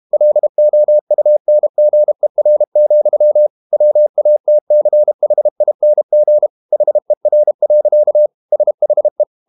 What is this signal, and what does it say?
Morse code, heard as a keyed tone, about 32 wpm, 600 Hz. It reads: LOUNGER, WATCHING HER. SHE
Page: Never 31wpm | CW med Gnister